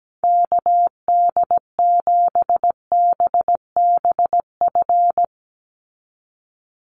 Morse code, heard as a keyed tone, about 17 wpm, 700 Hz. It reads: KD7BBF
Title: USA Call Signs 17wpm